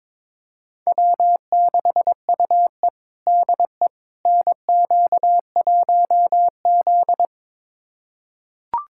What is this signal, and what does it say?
Morse code, heard as a keyed tone, about 22 wpm, 700 Hz. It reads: W6UE DE NQ1Z
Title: USA Callsign DE Callsign - 4 Characters - 22wpm